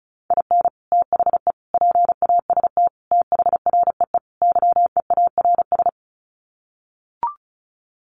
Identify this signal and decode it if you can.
Morse code, heard as a keyed tone, about 34 wpm, 700 Hz. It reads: IN THE PAST THREE YEARS
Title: Sets of 5 Words Encore2 35wpm